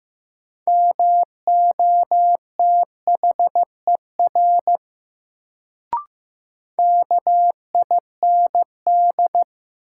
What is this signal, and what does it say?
Morse code, heard as a keyed tone, about 15 wpm, 700 Hz. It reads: MOTHER KIND
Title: Sentences from Top 200 Words - New Word Drill - 15wpm